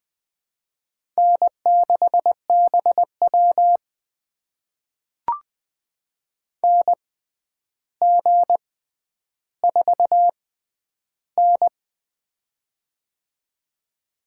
Morse code, heard as a keyed tone, about 20 wpm, spaced as for 10 wpm, 700 Hz. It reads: N6BW NG4N
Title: USA Call Signs - 4 Characters - 6x Character Spacing - 20wpm